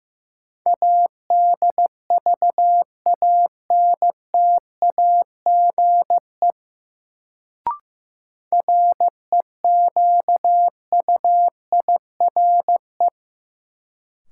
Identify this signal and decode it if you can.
Morse code, heard as a keyed tone, about 15 wpm, 700 Hz. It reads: ADVANTAGE REQUIRE
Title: Sentences from Top 900 Words - New Word Drill - 15wpm